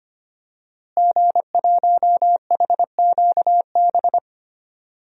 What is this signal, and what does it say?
Morse code, heard as a keyed tone, about 25 wpm, 700 Hz. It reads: G1HQB
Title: DXCC-100 Call Signs - 5 Characters - Repeated 3x - 25wpm